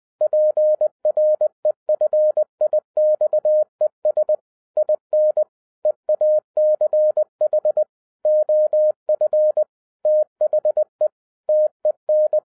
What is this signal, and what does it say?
Morse code, heard as a keyed tone, about 20 wpm, 610 Hz. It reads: PREFIXES IN EACH OF THE TEN